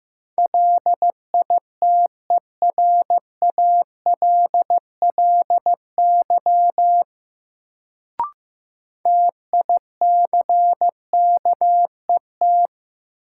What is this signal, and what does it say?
Morse code, heard as a keyed tone, about 15 wpm, 700 Hz. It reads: LITERALLY TICKET
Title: Sentences from Top 1400 Words - New Word Drill - 15wpm